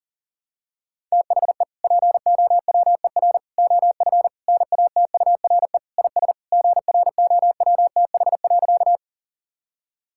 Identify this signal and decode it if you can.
Morse code, heard as a keyed tone, about 40 wpm, 700 Hz. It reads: THE POWER OF NATURE IS GROWTH.
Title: Sentences from Top 1000 Words - Review Part 5 - 40wpm